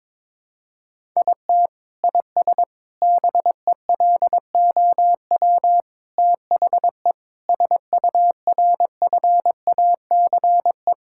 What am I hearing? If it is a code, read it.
Morse code, heard as a keyed tone, about 22 wpm, 700 Hz. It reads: IT IS BELOW THE SURFACE
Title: Sentences from Top 1000 Words - Review Part 5 - 22wpm